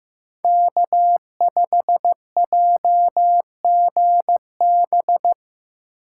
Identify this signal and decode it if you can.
Morse code, heard as a keyed tone, about 18 wpm, 700 Hz. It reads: K5JGB
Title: Speed Racing - USA 1x3 Call Signs 15wpm